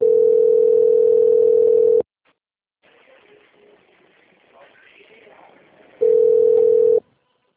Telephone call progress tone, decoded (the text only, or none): ringback tone